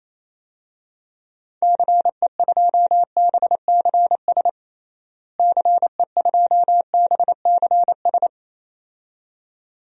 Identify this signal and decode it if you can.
Morse code, heard as a keyed tone, about 28 wpm, 700 Hz. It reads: CE2BCS CE2BCS
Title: DXCC-100 Call Signs - 6 Characters - Repeated 2x - 28wpm